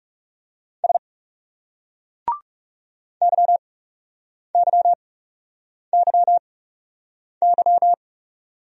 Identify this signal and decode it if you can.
Morse code, heard as a keyed tone, about 37 wpm, spaced as for 8 wpm, 700 Hz. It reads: S YYYY
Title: Speed-Racing Single Letters 30wpm